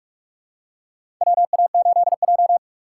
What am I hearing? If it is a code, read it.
Morse code, heard as a keyed tone, about 44 wpm, 700 Hz. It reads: WA8J
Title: USA Call Signs - 4 Characters - 6x Character Spacing - 45wpm